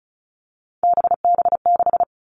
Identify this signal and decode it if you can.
Morse code, heard as a keyed tone, about 36 wpm, 700 Hz. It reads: BB6
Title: Three Challenging Letter-Number Combinations 35wpm